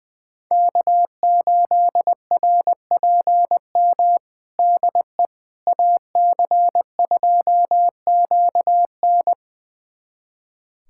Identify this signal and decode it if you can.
Morse code, heard as a keyed tone, about 20 wpm, 700 Hz. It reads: K8RPM DE AC2QN
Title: USA Callsign DE Callsign - 5 Characters - 20wpm